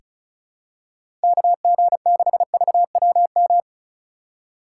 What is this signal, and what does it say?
Morse code, heard as a keyed tone, about 36 wpm, 690 Hz. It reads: KG6VWM
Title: USA Call Signs - 6 Characters - 3x Character Spacing - 35wpm